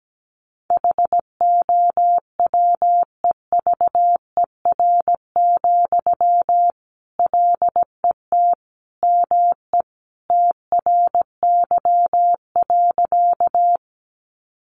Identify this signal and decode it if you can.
Morse code, heard as a keyed tone, about 17 wpm, 700 Hz. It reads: HOWEVER, LET ME TRY.
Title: Sentences from Top 600 Words 17wpm